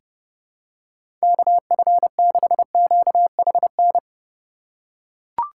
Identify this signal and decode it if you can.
Morse code, heard as a keyed tone, about 30 wpm, 700 Hz. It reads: KF6QHN
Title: USA Call Signs - 6 Characters - 3x Character Spacing - 30wpm